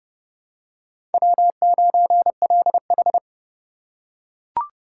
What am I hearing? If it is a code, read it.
Morse code, heard as a keyed tone, about 30 wpm, 700 Hz. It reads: W9LH
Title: USA Call Signs - 4 Characters - 4x Character Spacing - 30wpm